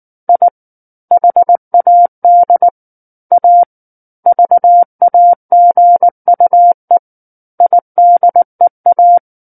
Morse code, heard as a keyed tone, about 19 wpm, 700 Hz. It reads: I HAD A VAGUE IDEA
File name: Chapter 10 - 19 WPM